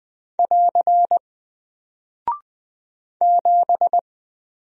Morse code, heard as a keyed tone, about 20 wpm, 700 Hz. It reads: <AR> 7
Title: Single Character (Letter:Number:Punctuation:Prosign) Alt 20wpm